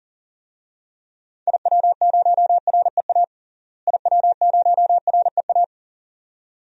Morse code, heard as a keyed tone, about 40 wpm, 700 Hz. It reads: IW0REA IW0REA
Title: DXCC-100 Call Signs - 6 Characters - Repeated 2x - 40wpm